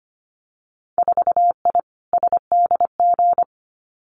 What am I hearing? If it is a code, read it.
Morse code, heard as a keyed tone, about 25 wpm, 700 Hz. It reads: 4I SDG
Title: Winter Field Day Exchanges Encore 25wpm